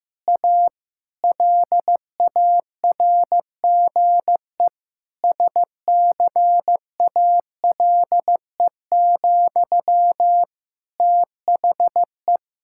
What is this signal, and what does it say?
Morse code, heard as a keyed tone, about 15 wpm, 700 Hz. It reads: A LARGE SCALE, THE
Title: Sentences from Top 1000 Words - Review Part 5 - 15wpm